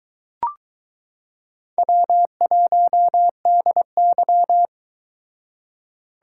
Morse code, heard as a keyed tone, about 23 wpm, 700 Hz. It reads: W1DY